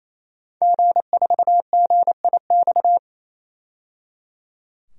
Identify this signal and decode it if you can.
Morse code, heard as a keyed tone, about 28 wpm, 700 Hz. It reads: G4GIX
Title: DXCC-100 Call Signs - 5 Characters - 28wpm